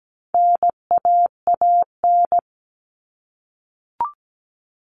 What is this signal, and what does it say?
Morse code, heard as a keyed tone, about 17 wpm, 700 Hz. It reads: NAAN
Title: All Cut Numbers - 4-Digit 17wpm